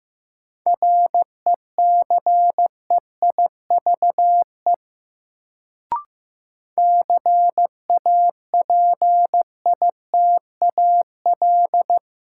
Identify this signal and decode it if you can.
Morse code, heard as a keyed tone, about 15 wpm, 700 Hz. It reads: RECEIVE CAPITAL
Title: Sentences from Top 1000 Words - New Word Drill - 15wpm